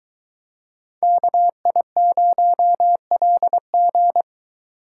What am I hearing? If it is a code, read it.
Morse code, heard as a keyed tone, about 22 wpm, 700 Hz. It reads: KI0LG